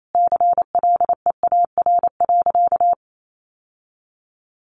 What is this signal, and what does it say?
Morse code, heard as a keyed tone, about 28 wpm, 700 Hz. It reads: CLEAR.
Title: Sentences from Top 1100 Words 28wpm